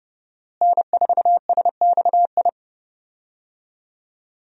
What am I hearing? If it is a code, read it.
Morse code, heard as a keyed tone, about 29 wpm, 700 Hz. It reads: N4SXI
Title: USA Call Signs - 5 Characters - Repeated 3x - 30wpm